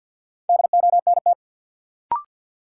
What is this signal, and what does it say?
Morse code, heard as a keyed tone, about 47 wpm, 700 Hz. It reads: DONT
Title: Sentences from Top 300 Words - New Word Drill - 50wpm